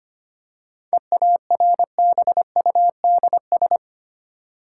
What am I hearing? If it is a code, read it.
Morse code, heard as a keyed tone, about 30 wpm, 700 Hz. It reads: EARBUDS
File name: practice / 25wpm / Difficult Words - Speed-Racing with Hints - 25wpm